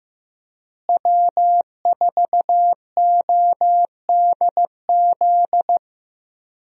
Morse code, heard as a keyed tone, about 15 wpm, 700 Hz. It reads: W4ODZ
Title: USA Call Signs - 5 Characters - Repeated 3x - 15wpm